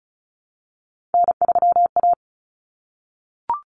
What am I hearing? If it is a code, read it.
Morse code, heard as a keyed tone, about 34 wpm, 700 Hz. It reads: N3A
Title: Common Cut Numbers - 3-Digit 35wpm